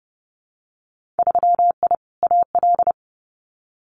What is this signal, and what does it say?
Morse code, heard as a keyed tone, about 32 wpm, 700 Hz. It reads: 3I AL
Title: Winter Field Day Exchanges 30wpm